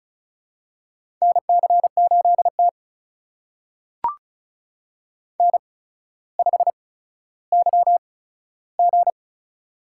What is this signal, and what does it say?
Morse code, heard as a keyed tone, about 36 wpm, spaced as for 15 wpm, 700 Hz. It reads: NC8T N5YG
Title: USA Call Signs - 4 Characters - 8x Character Spacing - 35wpm